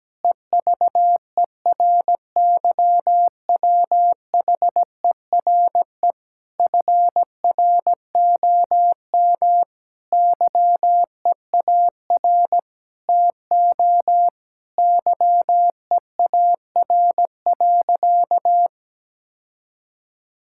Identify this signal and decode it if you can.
Morse code, heard as a keyed tone, about 17 wpm, 700 Hz. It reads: EVERYWHERE FROM YEAR TO YEAR.
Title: Sentences from Top 1600 Words 17wpm